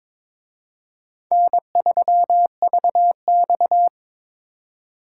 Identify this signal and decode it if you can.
Morse code, heard as a keyed tone, about 26 wpm, 700 Hz. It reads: N3VX
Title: Speed Racing - USA 1x2 Call Signs 22wpm